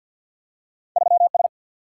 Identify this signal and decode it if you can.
Morse code, heard as a keyed tone, about 49 wpm, 700 Hz. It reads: 3 S
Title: Single-Letter Single-Number 50wpm